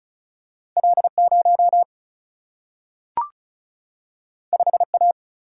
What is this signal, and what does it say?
Morse code, heard as a keyed tone, about 36 wpm, 700 Hz. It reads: L 0 5 A
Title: Single-Letter Single-Number 35wpm